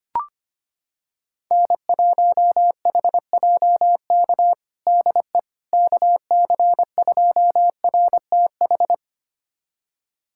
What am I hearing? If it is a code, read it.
Morse code, heard as a keyed tone, about 25 wpm, 700 Hz. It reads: N1HJK DE KC2RTH
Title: USA Callsign DE Callsign 25wpm